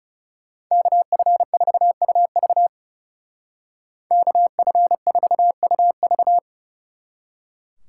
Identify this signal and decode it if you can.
Morse code, heard as a keyed tone, about 35 wpm, 700 Hz. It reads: KF4UV KF4UV
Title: Speed Racing - USA 2x2 Call Signs 30wpm